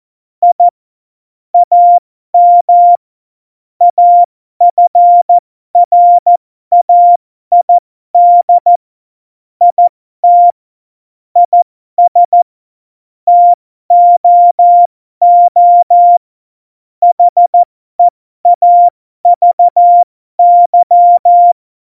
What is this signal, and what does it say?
Morse code, heard as a keyed tone, about 14 wpm, spaced as for 12 wpm, 700 Hz. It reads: I AM AFRAID IT IS TOO HEAVY
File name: Morsekurs / Witze